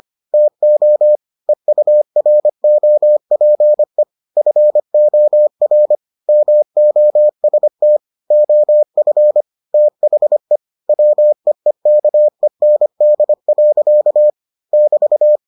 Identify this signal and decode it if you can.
Morse code, heard as a keyed tone, about 25 wpm, 600 Hz. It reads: TO EUROPE FOR MOST OF THE WEEKEND. <BT>